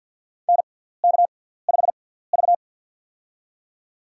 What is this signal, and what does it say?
Morse code, heard as a keyed tone, about 53 wpm, spaced as for 17 wpm, 700 Hz. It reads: NK5V
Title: USA Call Signs - 4 Characters - 6x Character Spacing - 50wpm